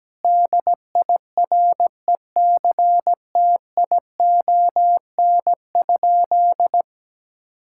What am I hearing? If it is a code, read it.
Morse code, heard as a keyed tone, about 17 wpm, 700 Hz. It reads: DIRECTION?
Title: Sentences from Top 1000 Words - Review Part 5 - 17wpm